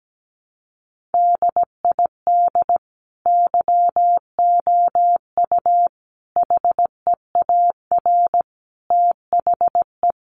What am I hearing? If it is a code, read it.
Morse code, heard as a keyed tone, about 17 wpm, 700 Hz. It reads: DID YOU HEAR THE
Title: Sentences from Top 1400 Words 17wpm